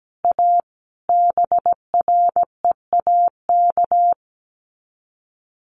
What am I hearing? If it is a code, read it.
Morse code, heard as a keyed tone, about 17 wpm, 700 Hz. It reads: A BREAK
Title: Sets of 5 Words 17wpm